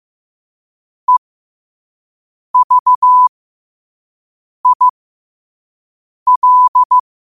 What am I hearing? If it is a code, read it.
Morse code, heard as a keyed tone, about 15 wpm, spaced as for 5 wpm, 1000 Hz. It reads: EVIL
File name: qotdfiles / 5wpm